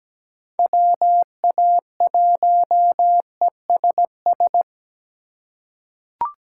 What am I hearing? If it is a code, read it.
Morse code, heard as a keyed tone, about 17 wpm, 700 Hz. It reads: WA1ESS
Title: USA Call Signs - 3x Character Spacing - 17wpm